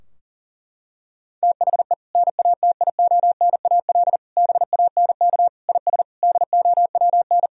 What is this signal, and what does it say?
Morse code, heard as a keyed tone, about 40 wpm, 700 Hz. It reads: THE NATIONAL BANK IS DOWN
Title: Sentences from Top 1000 Words - Review Part 4 - 40wpm